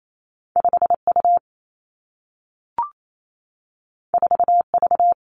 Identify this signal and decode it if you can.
Morse code, heard as a keyed tone, about 28 wpm, 700 Hz. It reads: 5 U 4 V
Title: Two Challenging Letter-Number Combinations 28wpm